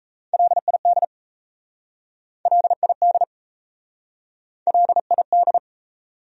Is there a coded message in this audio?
Yes: Morse code from 0.3 to 5.6 s, heard as a keyed tone, about 37 wpm, 700 Hz. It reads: LID LID LID